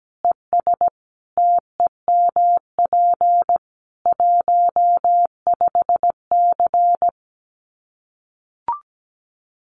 Morse code, heard as a keyed tone, about 17 wpm, 700 Hz. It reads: ES TEMP 15C
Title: Common QSO Phrases 17wpm